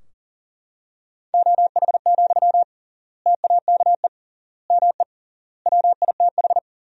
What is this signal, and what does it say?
Morse code, heard as a keyed tone, about 40 wpm, 700 Hz. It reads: OH, TAKE ME WITH
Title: 3x Word-Spacing - Sentences from Top 500 Words - 40wpm